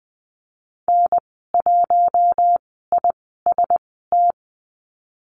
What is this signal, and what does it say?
Morse code, heard as a keyed tone, about 20 wpm, spaced as for 14 wpm, 700 Hz. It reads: N1IST
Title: 042 Lesson 8g - Number 1 - Call Signs 20wpm